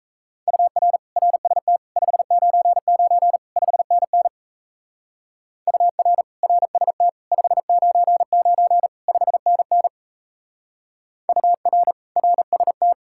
Morse code, heard as a keyed tone, about 34 wpm, 700 Hz. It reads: UR RST 599 5NN UR RST 599 5NN UR RST
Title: Speed-Racing Common QSO Phrases - 28wpm